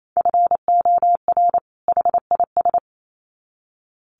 Morse code, heard as a keyed tone, about 28 wpm, 700 Hz. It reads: FOR HIS
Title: Sets of 2 Words 28wpm